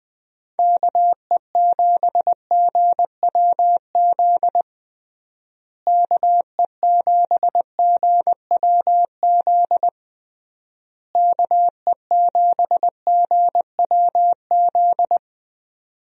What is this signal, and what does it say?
Morse code, heard as a keyed tone, about 21 wpm, 700 Hz. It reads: KE7GWZ KE7GWZ KE7GWZ
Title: USA Call Signs - 6 Characters - Repeated 3x - 20wpm